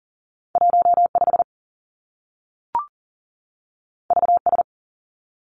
Morse code, heard as a keyed tone, about 40 wpm, 700 Hz. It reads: 1 5 V S
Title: Two Challenging Letter-Number Combinations 40wpm